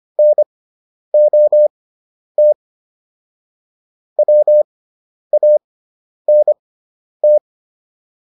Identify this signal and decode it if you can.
Morse code, heard as a keyed tone, about 25 wpm, spaced as for 10 wpm, 600 Hz. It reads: NOT WANT
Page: intermediate-practice-files